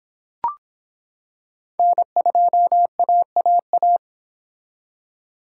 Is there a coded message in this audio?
Yes: Morse code from 1.8 to 4.0 s, heard as a keyed tone, about 26 wpm, 700 Hz. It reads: N2AAA